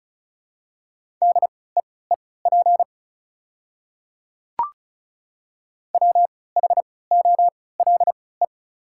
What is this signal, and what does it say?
Morse code, heard as a keyed tone, about 36 wpm, spaced as for 20 wpm, 690 Hz. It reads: DEEP WHOLE
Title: 3x Character Spacing - Top 500 Words - 35wpm